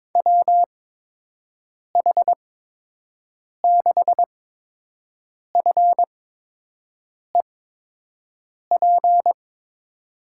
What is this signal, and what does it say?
Morse code, heard as a keyed tone, about 22 wpm, spaced as for 6 wpm, 700 Hz. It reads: WH6FEP
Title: USA Call Signs - 6 Characters - 8x Character Spacing - 22wpm